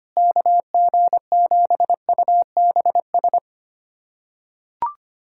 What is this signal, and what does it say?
Morse code, heard as a keyed tone, about 25 wpm, 700 Hz. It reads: KG7UBS
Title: USA Callsign DE Callsign 25wpm